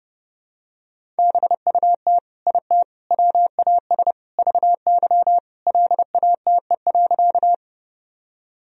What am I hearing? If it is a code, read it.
Morse code, heard as a keyed tone, about 30 wpm, 700 Hz. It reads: BUT IT WAS VY LATE.
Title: Sentences from Top 300 Words and Abbreviations 30wpm